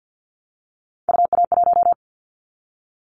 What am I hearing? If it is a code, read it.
Morse code, heard as a keyed tone, about 50 wpm, 700 Hz. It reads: UA1
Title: Callsign Prefixes for Top 25 DXCC - Repeated 3x - 50wpm